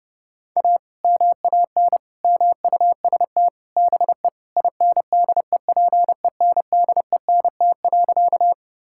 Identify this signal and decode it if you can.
Morse code, heard as a keyed tone, about 32 wpm, 700 Hz. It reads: A MAN MUST BE INDEPENDENT.
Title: Sentences from Top 1000 Words - Review Part 6 - 30wpm